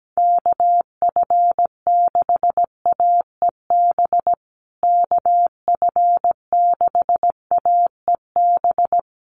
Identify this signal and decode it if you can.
Morse code, heard as a keyed tone, about 17 wpm, 700 Hz. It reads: KF6AEB KF6AEB
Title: USA Call Signs Repeated 3x 17wpm